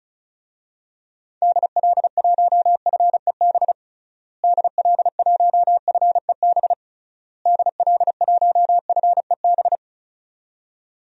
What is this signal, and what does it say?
Morse code, heard as a keyed tone, about 36 wpm, 700 Hz. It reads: DL1FEB DL1FEB DL1FEB
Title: DXCC-100 Call Signs - 6 Characters - Repeated 3x - 35wpm